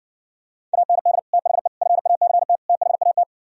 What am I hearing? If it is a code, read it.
Morse code, heard as a keyed tone, about 58 wpm, 700 Hz. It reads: AND THE FACT THAT